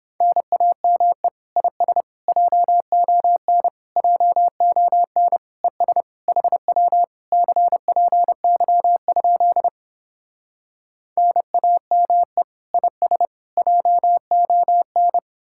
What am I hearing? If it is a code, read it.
Morse code, heard as a keyed tone, about 30 wpm, 700 Hz. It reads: NAME IS JON JON ES HW CPY? NAME IS JON